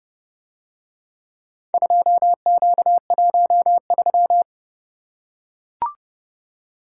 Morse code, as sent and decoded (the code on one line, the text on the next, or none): ..--- --.- .---- ...--
2Q13